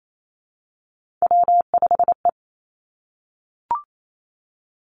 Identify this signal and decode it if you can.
Morse code, heard as a keyed tone, about 28 wpm, 700 Hz. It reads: W5E
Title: Three Challenging Letter-Number Combinations 28wpm